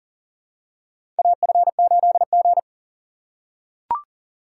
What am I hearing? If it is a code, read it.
Morse code, heard as a keyed tone, about 40 wpm, 700 Hz. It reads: AF8G
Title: USA Call Signs - 4 Characters - 4x Character Spacing - 40wpm